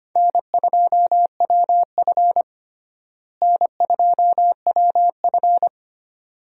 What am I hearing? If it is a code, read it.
Morse code, heard as a keyed tone, about 25 wpm, 700 Hz. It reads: N2WF N2WF
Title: USA Call Signs - 4 Characters - Repeated 2x - 25wpm